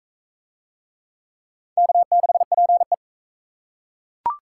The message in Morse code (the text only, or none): K6PE